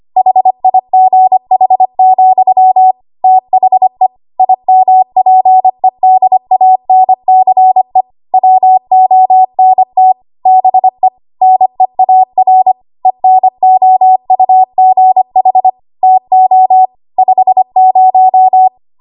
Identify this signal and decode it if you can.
Morse code, heard as a keyed tone, about 25 wpm, 750 Hz. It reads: HIGH, THE IMPEDANCE WONT BE NEAR ENOUGH TO 50
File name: Archive / 25 WPM / 25WPM